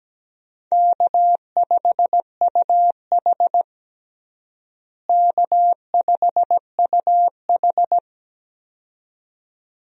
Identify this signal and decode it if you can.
Morse code, heard as a keyed tone, about 17 wpm, 700 Hz. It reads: K5UH K5UH
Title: USA Call Signs - 4 Characters - Repeated 2x - 17wpm